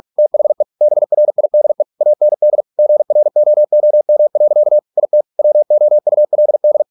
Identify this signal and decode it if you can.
Morse code, heard as a keyed tone, about 46 wpm, 600 Hz. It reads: THE BRIDE AND GROOM. IT WOULD